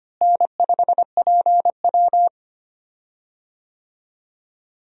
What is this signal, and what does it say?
Morse code, heard as a keyed tone, about 25 wpm, 700 Hz. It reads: N5PW
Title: USA Call Signs - 4 Characters - Repeated 2x - 25wpm